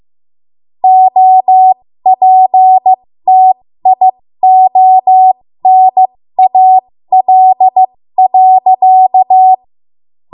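Morse code, heard as a keyed tone, about 15 wpm, spaced as for 13 wpm, 750 Hz. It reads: OPTIONAL.
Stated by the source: 13 wpm Morse code practice archive transmission.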